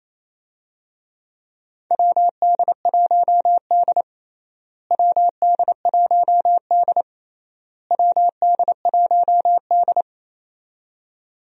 Morse code, heard as a keyed tone, about 28 wpm, 700 Hz. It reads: WD1D WD1D WD1D
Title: USA Call Signs - 4 Characters - Repeated 3x - 28wpm